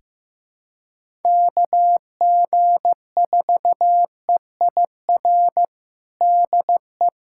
Morse code, heard as a keyed tone, about 15 wpm, 700 Hz. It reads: KG4EIR DE
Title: USA Callsign DE Callsign 15wpm